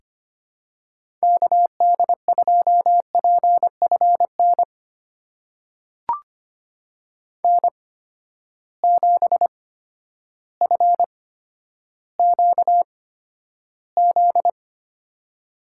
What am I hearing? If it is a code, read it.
Morse code, heard as a keyed tone, about 25 wpm, spaced as for 11 wpm, 700 Hz. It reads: KD2PFN N7FQZ
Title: USA Call Signs - 8x Character Spacing - 25wpm